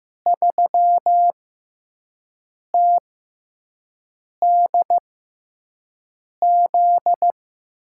Morse code, heard as a keyed tone, about 15 wpm, spaced as for 5 wpm, 700 Hz. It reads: 3TDZ
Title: USA Call Signs - 6 Characters - 6x Character Spacing - 15wpm